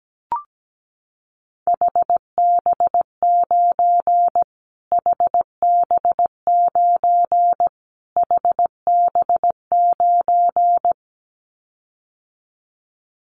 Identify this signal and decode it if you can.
Morse code, heard as a keyed tone, about 17 wpm, 700 Hz. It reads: HB9 HB9 HB9
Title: Callsign Prefixes for Top 25 DXCC - Repeated 3x - 17wpm